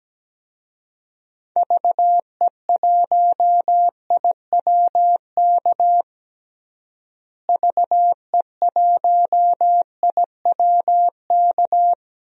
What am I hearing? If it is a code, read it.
Morse code, heard as a keyed tone, about 17 wpm, 700 Hz. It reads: VE1IWK VE1IWK
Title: DXCC-100 Call Signs - 6 Characters - Repeated 2x - 17wpm